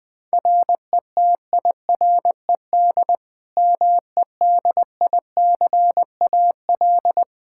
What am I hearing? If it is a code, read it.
Morse code, heard as a keyed tone, about 20 wpm, 700 Hz. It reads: RETIRED MEDICAL
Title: Practice QSO - Exchange 3 - More Info - 20wpm